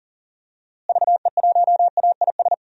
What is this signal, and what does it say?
Morse code, heard as a keyed tone, about 39 wpm, 700 Hz. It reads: VE1AIS
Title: DXCC-100 Call Signs - 6 Characters - Repeated 2x - 40wpm